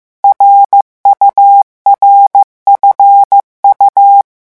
Morse code, heard as a keyed tone, about 15 wpm, 800 Hz. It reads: RURFU
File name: morse mp3 / revision11 lettres vitesse 14